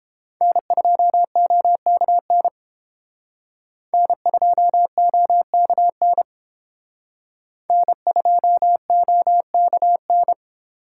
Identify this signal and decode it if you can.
Morse code, heard as a keyed tone, about 31 wpm, 700 Hz. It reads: N2OKN N2OKN N2OKN